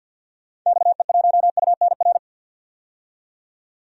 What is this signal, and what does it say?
Morse code, heard as a keyed tone, about 50 wpm, 700 Hz. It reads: XE1UNR
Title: DXCC-100 Call Signs - 6 Characters - 50wpm